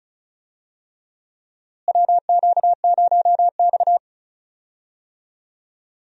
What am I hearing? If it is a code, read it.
Morse code, heard as a keyed tone, about 34 wpm, 700 Hz. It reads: WQ0X
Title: USA Call Signs - 4 Characters - Repeated 3x - 35wpm